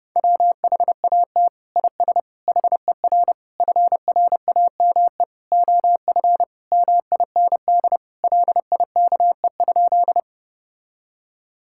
Morse code, heard as a keyed tone, about 30 wpm, 700 Hz. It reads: WHAT IS HER FRAME OF MIND LIKE?
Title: Sentences from Top 1000 Words - Review Part 6 - 30wpm